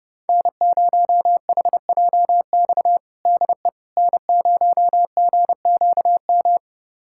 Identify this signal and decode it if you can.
Morse code, heard as a keyed tone, about 30 wpm, 700 Hz. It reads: N0HJX DE N0GQM
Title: USA Callsign DE Callsign - 5 Characters - 30wpm